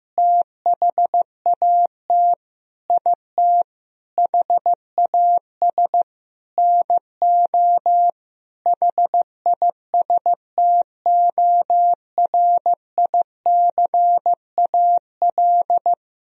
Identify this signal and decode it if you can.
Morse code, heard as a keyed tone, about 16 wpm, 700 Hz. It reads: THAT IT HAS NO HISTORICAL
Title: Sentences from Top 1000 Words - Review Part 4 - 15wpm